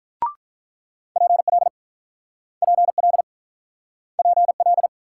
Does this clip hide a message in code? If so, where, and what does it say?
Yes: Morse code from 1.2 to 4.9 s, heard as a keyed tone, about 49 wpm, 700 Hz. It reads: PL PL PL